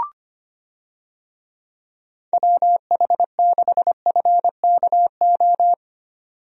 Morse code, heard as a keyed tone, about 25 wpm, 700 Hz. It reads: WH6FKO
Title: USA Call Signs - 6 Characters - 25wpm